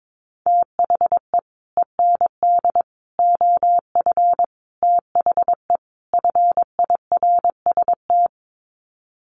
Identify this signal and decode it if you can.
Morse code, heard as a keyed tone, about 22 wpm, 700 Hz. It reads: THE END OF THE FIRST
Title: Sets of 5 Words Encore 22wpm